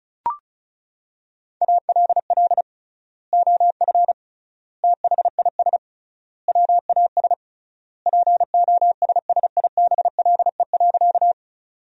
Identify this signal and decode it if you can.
Morse code, heard as a keyed tone, about 36 wpm, 700 Hz. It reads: ALL OF THIS WAS POSSIBLE.
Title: 3x Word-Spacing - Sentences from Top 500 Words - 35wpm